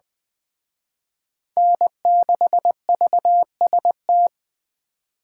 Morse code, heard as a keyed tone, about 21 wpm, 700 Hz. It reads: N6VST
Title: Speed Racing - USA 1x3 Call Signs 17wpm